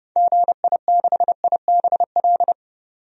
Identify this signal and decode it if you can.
Morse code, heard as a keyed tone, about 30 wpm, 700 Hz. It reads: GI6IBL
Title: DXCC-100 Call Signs - 6 Characters - 30wpm